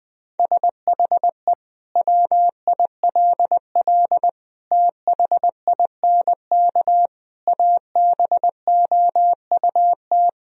Morse code, heard as a keyed tone, about 21 wpm, 700 Hz. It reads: SHE WILL THINK ABOUT
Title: Sentences from Top 1000 Words - Review Part 1 - 20wpm